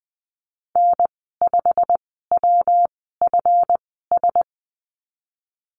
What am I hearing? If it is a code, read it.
Morse code, heard as a keyed tone, about 20 wpm, spaced as for 15 wpm, 700 Hz. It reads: N5WFS
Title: 147 Lesson 24g - Letter y - Call Signs 20wpm